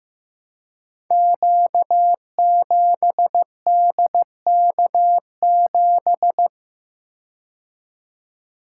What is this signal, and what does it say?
Morse code, heard as a keyed tone, about 15 wpm, 700 Hz. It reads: Q7DK7
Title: Code Groups - 5 Letter:Numbers - 15wpm